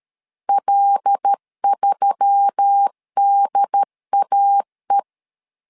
Morse code, heard as a keyed tone, about 13 wpm, 800 Hz. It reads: L3DAE
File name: LEZIONI / 5 LBVC / 5 L 12wpm 1913